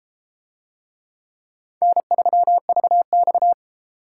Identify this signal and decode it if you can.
Morse code, heard as a keyed tone, about 31 wpm, 700 Hz. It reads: N3VX